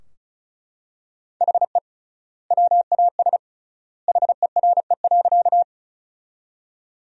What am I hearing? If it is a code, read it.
Morse code, heard as a keyed tone, about 34 wpm, 700 Hz. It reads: HE WAS HERE.
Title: 3x Word-Spacing - Sentences from Top 200 Words - 35wpm